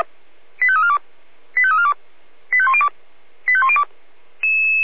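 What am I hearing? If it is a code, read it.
Five-tone selective call (ZVEI), heard as quick runs of five tones, tones of about 70 ms. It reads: 85342 85342 85102 85102
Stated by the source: ZVEI five-tone selective call received on SDR.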